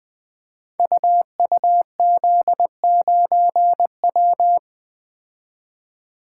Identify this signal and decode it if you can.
Morse code, heard as a keyed tone, about 20 wpm, 700 Hz. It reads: UUZ9W
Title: Code Groups - 5 Letter:Numbers - 20wpm